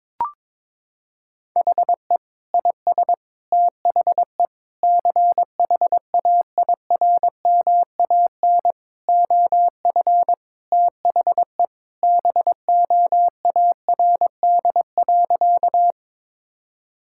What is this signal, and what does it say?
Morse code, heard as a keyed tone, about 22 wpm, 700 Hz. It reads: HE IS THE CHAIRMAN OF THE BOARD.
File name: practice / 22wpm / Sentences with 3-Word Phrases - 22wpm